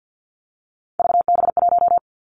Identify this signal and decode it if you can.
Morse code, heard as a keyed tone, about 50 wpm, 700 Hz. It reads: VB1
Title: Three Challenging Letter-Number Combinations 50wpm